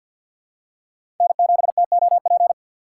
Morse code, heard as a keyed tone, about 49 wpm, 700 Hz. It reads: N7TOP
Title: USA Call Signs - 3x Character Spacing - 50wpm